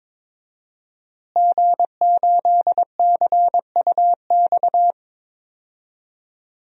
Morse code, heard as a keyed tone, about 22 wpm, 700 Hz. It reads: G8CUX
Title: DXCC-100 Call Signs - 5 Characters - Repeated 2x - 22wpm